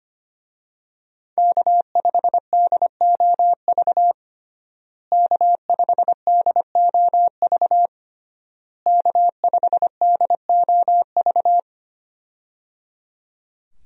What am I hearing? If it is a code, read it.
Morse code, heard as a keyed tone, about 25 wpm, 700 Hz. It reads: K5DOV K5DOV K5DOV
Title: USA Call Signs - 5 Characters - Repeated 3x - 25wpm